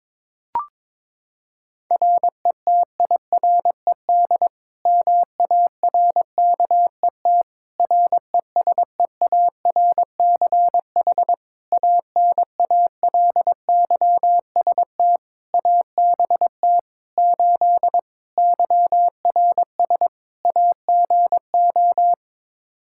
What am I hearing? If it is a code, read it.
Morse code, heard as a keyed tone, about 22 wpm, 700 Hz. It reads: RETIRED MARKET RESEARCH ANALYST ABT 8 YRS AGO
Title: Practice QSO - Exchange 3 - More Info - 22wpm